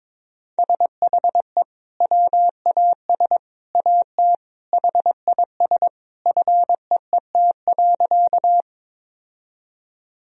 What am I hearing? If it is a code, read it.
Morse code, heard as a keyed tone, about 22 wpm, 700 Hz. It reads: SHE WAS AT HIS FEET.
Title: Sentences from Top 1000 Words - Review Part 1 - 22wpm